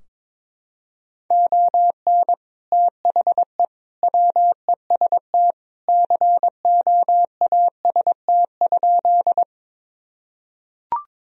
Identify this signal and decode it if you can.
Morse code, heard as a keyed tone, about 22 wpm, 700 Hz. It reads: ON THE WEST COAST?
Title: Sentences from Top 1000 Words - Review Part 4 - 22wpm